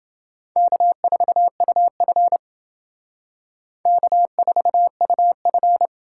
Morse code, heard as a keyed tone, about 29 wpm, 700 Hz. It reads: K4UF K4UF